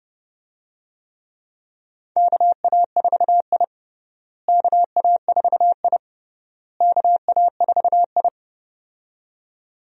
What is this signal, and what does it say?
Morse code, heard as a keyed tone, about 32 wpm, 700 Hz. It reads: KA4I KA4I KA4I
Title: USA Call Signs - 4 Characters - Repeated 3x - 30wpm